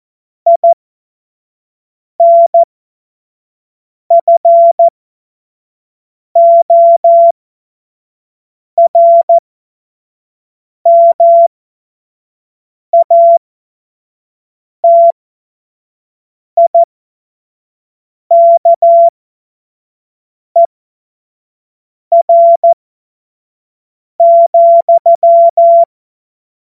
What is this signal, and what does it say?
Morse code, heard as a keyed tone, about 14 wpm, spaced as for 5 wpm, 680 Hz. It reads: INFORMATIKER,
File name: Morsekurs / Witze / Witz 21. Dezember 2021, 25 F70